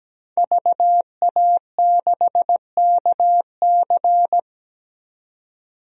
Morse code, heard as a keyed tone, about 17 wpm, 700 Hz. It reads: VA6KC
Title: DXCC-100 Call Signs - 5 Characters - 17wpm